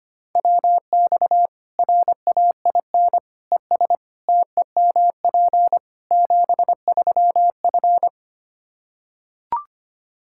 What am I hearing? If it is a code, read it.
Morse code, heard as a keyed tone, about 25 wpm, 700 Hz. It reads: WX RAIN ES TEMP 73F
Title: Practice QSO - Exchange 2 - Info - 25wpm